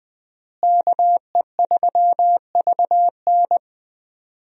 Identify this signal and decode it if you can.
Morse code, heard as a keyed tone, about 21 wpm, 700 Hz. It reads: KE3VN
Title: USA Call Signs - 5 Characters - Repeated 3x - 20wpm